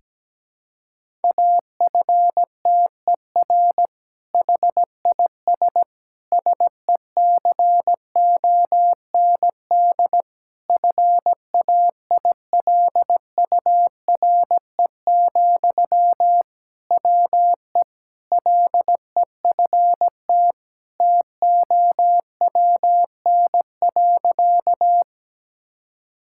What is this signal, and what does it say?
Morse code, heard as a keyed tone, about 17 wpm, 700 Hz. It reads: AFTER HIS SECOND FAILURE, WE LEFT TOWN.
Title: Sentences from Top 1000 Words - Review Part 6 - 17wpm